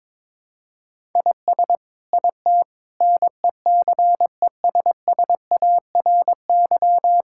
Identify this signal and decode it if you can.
Morse code, heard as a keyed tone, about 22 wpm, 700 Hz. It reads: IS IT NECESSARY
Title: Sentences from Top 1000 Words - Review Part 6 - 22wpm